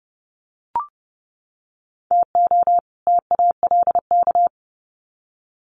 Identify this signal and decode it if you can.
Morse code, heard as a keyed tone, about 30 wpm, 700 Hz. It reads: TO TALK
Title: Sets of 2 Words 30wpm